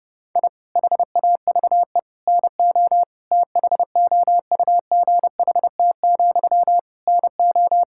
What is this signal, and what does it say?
Morse code, heard as a keyed tone, about 30 wpm, 700 Hz. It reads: I HAVE NO THOUGHT, NO
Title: Sentences from Top 1000 Words - Review Part 6 - 30wpm